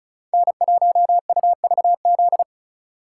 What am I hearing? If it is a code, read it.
Morse code, heard as a keyed tone, about 36 wpm, 700 Hz. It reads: N1UVZ
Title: USA Call Signs - 4x Character Spacing - 35wpm